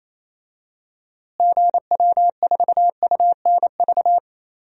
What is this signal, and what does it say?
Morse code, heard as a keyed tone, about 28 wpm, 700 Hz. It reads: GW4UNV
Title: DXCC-100 Call Signs - 6 Characters - Repeated 3x - 28wpm